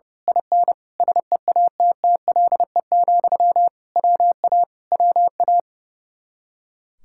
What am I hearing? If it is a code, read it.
Morse code, heard as a keyed tone, about 28 wpm, 700 Hz. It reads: IN SEATTLE, WA WA
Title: Speed-Racing Common QSO Phrases - 30wpm